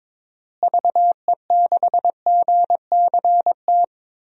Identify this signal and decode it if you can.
Morse code, heard as a keyed tone, about 23 wpm, 700 Hz. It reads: VE6GCT
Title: DXCC-100 Call Signs - 6 Characters - Repeated 2x - 22wpm